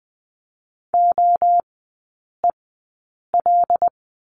Morse code, heard as a keyed tone, about 20 wpm, spaced as for 8 wpm, 700 Hz. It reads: OEL
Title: 074 Lesson 13e - Letter L - Three Characters 20wpm